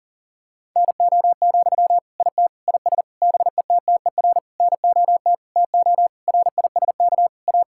Morse code, heard as a keyed tone, about 37 wpm, 700 Hz. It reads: NO, IT IS BETTER NOT TO RISK A
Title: Sentences from Top 1600 Words 40wpm